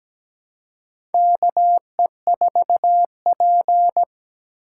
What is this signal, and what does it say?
Morse code, heard as a keyed tone, about 17 wpm, 700 Hz. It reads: KE4P
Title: USA Call Signs - 4 Characters - 6x Character Spacing - 17wpm